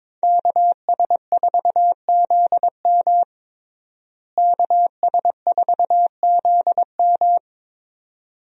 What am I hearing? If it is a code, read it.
Morse code, heard as a keyed tone, about 22 wpm, 700 Hz. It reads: KS4ZM KS4ZM
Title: USA Call Signs - 5 Characters - Repeated 2x - 22wpm